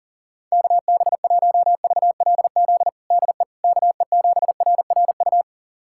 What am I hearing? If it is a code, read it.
Morse code, heard as a keyed tone, about 39 wpm, 700 Hz. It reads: KB1VLZ DE KE7RRU
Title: USA Callsign DE Callsign 40wpm